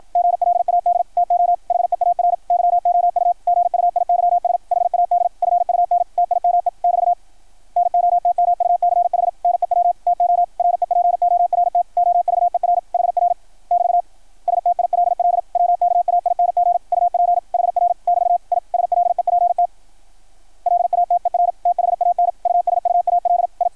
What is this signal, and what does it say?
Morse code, heard as a keyed tone, about 53 wpm, 690 Hz, with radio noise. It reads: GLAD TO HEAR YOU GUIYS HAD FUN TIME <BT> NOTMUCH NEW TO REPORT OVER HR <BT> STILL WKAING UP HR <BT> I SLEPT LATER THAN USUAL I
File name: archived / CRSnet